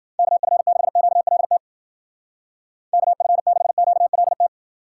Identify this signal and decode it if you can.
Morse code, heard as a keyed tone, about 49 wpm, 700 Hz. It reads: KF6QLT KF6QLT
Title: Speed Racing - USA Call Signs 40wpm